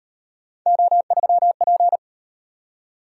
Morse code, heard as a keyed tone, about 38 wpm, 700 Hz. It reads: O3P